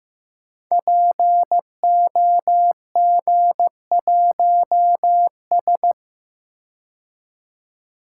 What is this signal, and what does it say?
Morse code, heard as a keyed tone, about 15 wpm, 700 Hz. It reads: POG1S
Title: Code Groups - 5 Letter:Numbers - 15wpm